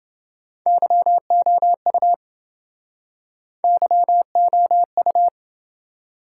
Morse code, heard as a keyed tone, about 28 wpm, 700 Hz. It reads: YOU YOU